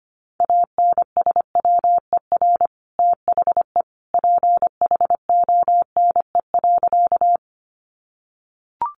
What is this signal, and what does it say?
Morse code, heard as a keyed tone, about 25 wpm, 700 Hz. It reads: ANSWER THE PHONE.
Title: Sentences from Top 1100 Words 25wpm